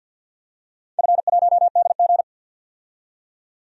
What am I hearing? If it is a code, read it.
Morse code, heard as a keyed tone, about 51 wpm, 700 Hz. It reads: F1DG
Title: DXCC-100 Call Signs - 4 Characters - 50wpm